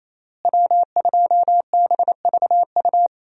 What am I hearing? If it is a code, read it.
Morse code, heard as a keyed tone, about 28 wpm, 700 Hz. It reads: W2BVU
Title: USA Call Signs - 5 Characters - Repeated 2x - 28wpm